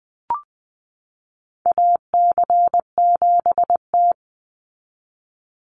Morse code, heard as a keyed tone, about 20 wpm, 700 Hz. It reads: AC7T
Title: Call Signs Repeated 2x - 20wpm